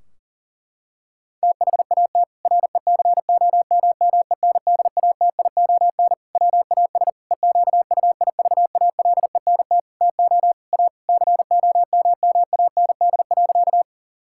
Morse code, heard as a keyed tone, about 40 wpm, 700 Hz. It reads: THAT RECOMMENDATION WAS EQUIVALENT TO A COMMAND.
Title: Sentences from Top 1600 Words 40wpm